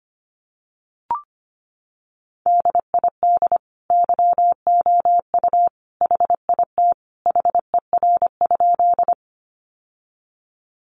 Morse code, heard as a keyed tone, about 25 wpm, 700 Hz. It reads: DID YOU HIT HER?
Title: Sentences from Top 1200 Words 25wpm